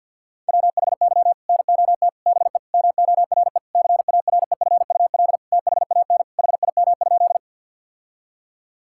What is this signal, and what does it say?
Morse code, heard as a keyed tone, about 50 wpm, 700 Hz. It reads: WHY NOT BE MORE CAREFUL THAN HIM?
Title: Sentences from Top 1600 Words 50wpm